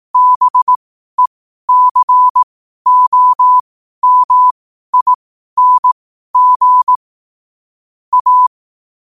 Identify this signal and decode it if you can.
Morse code, heard as a keyed tone, about 18 wpm, spaced as for 12 wpm, 1000 Hz. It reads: BECOMING A
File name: qotdfiles / 12wpm